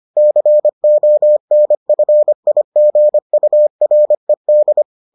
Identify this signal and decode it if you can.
Morse code, heard as a keyed tone, about 25 wpm, 600 Hz. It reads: CONFIGURED